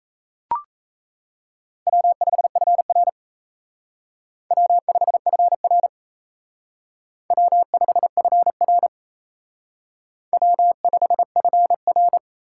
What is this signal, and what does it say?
Morse code, heard as a keyed tone, about 37 wpm, 700 Hz. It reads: W5FR W5FR W5FR W5FR